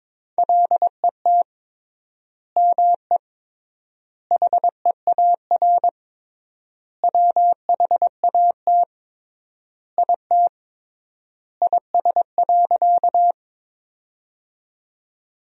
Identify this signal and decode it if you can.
Morse code, heard as a keyed tone, about 22 wpm, 700 Hz. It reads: LET ME HEAR WHAT IT IS.
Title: 3x Word-Spacing - Sentences from Top 300 Words - 22wpm